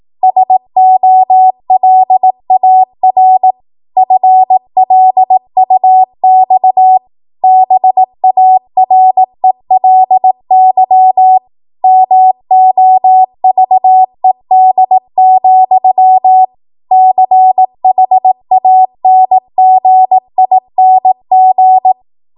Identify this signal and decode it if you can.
Morse code, heard as a keyed tone, about 18 wpm, 750 Hz. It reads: SOLAR FLUX BARELY MOVED, CHANGING
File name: Archive / 18 WPM / 18WPM